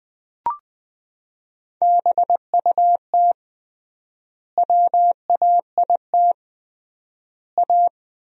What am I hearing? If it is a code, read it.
Morse code, heard as a keyed tone, about 20 wpm, 700 Hz. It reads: BUT WAIT A
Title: 3x Word-Spacing - Sentences from Top 500 Words - 20wpm